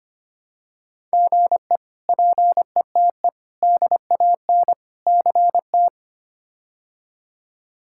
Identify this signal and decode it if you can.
Morse code, heard as a keyed tone, about 25 wpm, 700 Hz. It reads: GE PETE DAN CT
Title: K1USN SST QSO Practice - 25wpm